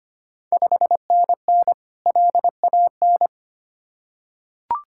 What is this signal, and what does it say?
Morse code, heard as a keyed tone, about 25 wpm, 700 Hz. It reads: 5NN LAN
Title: Ontario QSO Party Exchange 25wpm